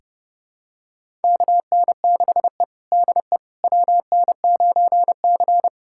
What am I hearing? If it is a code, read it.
Morse code, heard as a keyed tone, about 32 wpm, 700 Hz. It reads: KN6E DE WN9C
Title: USA Callsign DE Callsign - 4 Characters - 30wpm